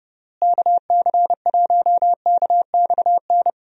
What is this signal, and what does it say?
Morse code, heard as a keyed tone, about 30 wpm, 700 Hz. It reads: KC1KXN
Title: USA Call Signs - 6 Characters - Repeated 2x - 30wpm